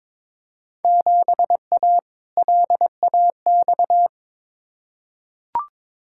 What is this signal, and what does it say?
Morse code, heard as a keyed tone, about 22 wpm, 700 Hz. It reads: 7A LAX
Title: Field Day Exchanges - 22wpm